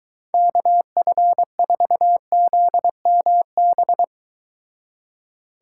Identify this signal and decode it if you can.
Morse code, heard as a keyed tone, about 23 wpm, 700 Hz. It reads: KF4ZMB